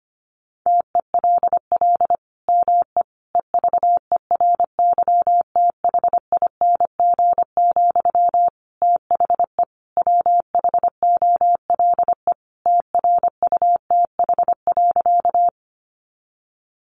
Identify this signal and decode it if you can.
Morse code, heard as a keyed tone, about 25 wpm, 700 Hz. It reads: TELL ME EVERYTHING, THE WHOLE TRUTH.
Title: Sentences from Top 800 Words 25wpm